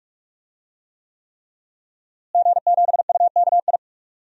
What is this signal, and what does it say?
Morse code, heard as a keyed tone, about 46 wpm, 700 Hz. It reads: G7UKI
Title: DXCC-100 Call Signs - 5 Characters - Repeated 3x - 45wpm